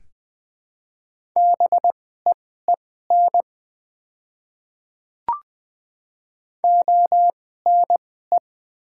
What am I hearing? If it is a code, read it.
Morse code, heard as a keyed tone, about 21 wpm, spaced as for 15 wpm, 700 Hz. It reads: BEEN ONE
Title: 2x Character Spacing - Top 100 Words - 20wpm